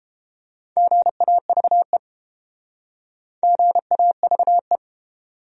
Morse code, heard as a keyed tone, about 31 wpm, 700 Hz. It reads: GAVE GAVE